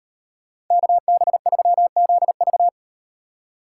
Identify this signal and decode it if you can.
Morse code, heard as a keyed tone, about 38 wpm, 700 Hz. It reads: KB3ZV